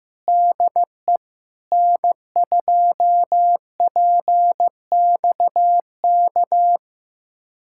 Morse code, heard as a keyed tone, about 16 wpm, 700 Hz. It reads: DE N2PXK
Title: USA Callsign DE Callsign - 5 Characters - 15wpm